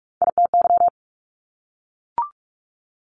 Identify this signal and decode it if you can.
Morse code, heard as a keyed tone, about 46 wpm, 690 Hz. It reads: ITY
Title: Top Three Letter Combinations in Words 45wpm